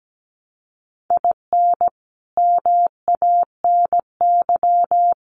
Morse code, heard as a keyed tone, about 17 wpm, 700 Hz. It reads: IN MANY
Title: Sets of 5 Words Encore 17wpm